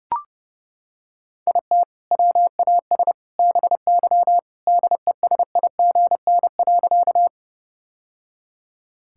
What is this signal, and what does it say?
Morse code, heard as a keyed tone, about 32 wpm, 700 Hz. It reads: IT WAS BY DESIGN.
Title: Sentences from Top 1000 Words - Review Part 5 - 30wpm